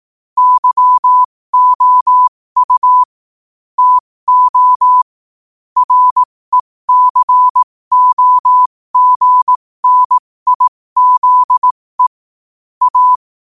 Morse code, heard as a keyed tone, about 18 wpm, spaced as for 15 wpm, 1000 Hz. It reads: YOU TO RECOGNIZE A
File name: 2024-11-18-15wpm